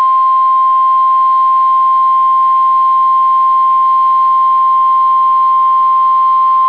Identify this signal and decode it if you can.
Emergency Alert System (SAME) tones (the 1050 Hz attention tone). It is heard from 0.4 to 5.9 s.